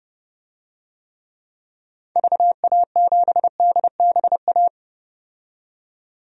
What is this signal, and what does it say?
Morse code, heard as a keyed tone, about 30 wpm, 700 Hz. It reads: VA7DBA
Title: DXCC-100 Call Signs - 6 Characters - 30wpm